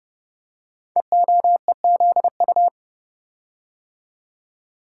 Morse code, heard as a keyed tone, about 30 wpm, 700 Hz. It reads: EOEZU
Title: Code Groups - 5 Letters - 30wpm